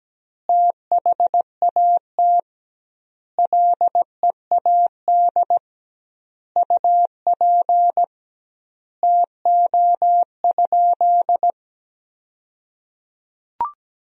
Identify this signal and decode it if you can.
Morse code, heard as a keyed tone, about 17 wpm, 700 Hz. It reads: THAT LEAD UP TO?
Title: 2x Word-Spacing - Sentences from Top 500 Words - 17wpm